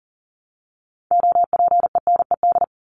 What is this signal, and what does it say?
Morse code, heard as a keyed tone, about 40 wpm, 700 Hz. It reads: OPENED
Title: ED Suffix - Top 100 Words with Base Words 40wpm